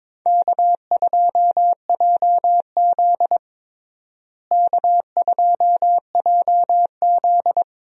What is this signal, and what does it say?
Morse code, heard as a keyed tone, about 22 wpm, 700 Hz. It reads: K2JZ K2JZ
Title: USA Call Signs - 4 Characters - Repeated 3x - 22wpm